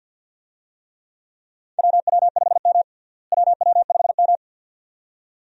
Morse code, heard as a keyed tone, about 48 wpm, 700 Hz. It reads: WW5M WW5M
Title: USA Call Signs - 4 Characters - Repeated 2x - 50wpm